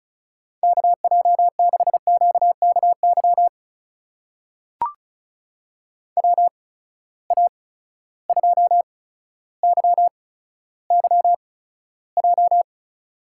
Morse code, heard as a keyed tone, about 36 wpm, spaced as for 15 wpm, 700 Hz. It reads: KJ6QKY WA2YYJ
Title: USA Call Signs - 8x Character Spacing - 35wpm